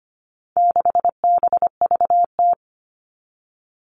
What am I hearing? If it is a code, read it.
Morse code, heard as a keyed tone, about 25 wpm, 700 Hz. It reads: 6BVT
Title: All Cut Numbers - 4-Digit 25wpm